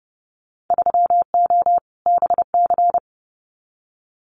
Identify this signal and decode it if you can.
Morse code, heard as a keyed tone, about 30 wpm, 700 Hz. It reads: 3O BC
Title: Winter Field Day Exchanges Encore 30wpm